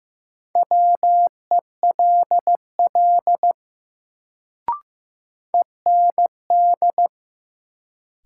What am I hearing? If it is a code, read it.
Morse code, heard as a keyed tone, about 15 wpm, 700 Hz. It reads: WELL END
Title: Sentences from Top 200 Words - New Word Drill - 15wpm